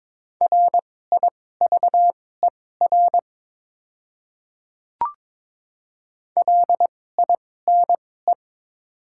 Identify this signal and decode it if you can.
Morse code, heard as a keyed tone, about 22 wpm, spaced as for 16 wpm, 700 Hz. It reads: RIVER LINE
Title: 2x Character Spacing - Top 500 Words - 22wpm